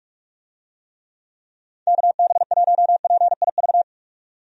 Morse code, heard as a keyed tone, about 46 wpm, 700 Hz. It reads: KB1PIV
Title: USA Call Signs - 6 Characters - 45wpm